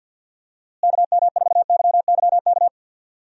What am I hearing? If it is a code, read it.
Morse code, heard as a keyed tone, about 53 wpm, 700 Hz. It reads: KM4YYK
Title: USA Call Signs - 6 Characters - Repeated 2x - 50wpm